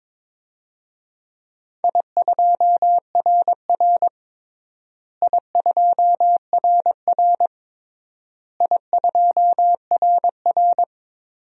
Morse code, heard as a keyed tone, about 22 wpm, 700 Hz. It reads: I2RR I2RR I2RR
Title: DXCC-100 Call Signs - 4 Characters - Repeated 3x - 22wpm